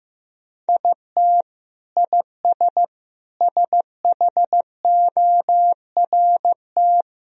Morse code, heard as a keyed tone, about 15 wpm, 700 Hz. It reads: IT IS SHORT
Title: Sentences from Top 1000 Words - Review Part 2 - 15wpm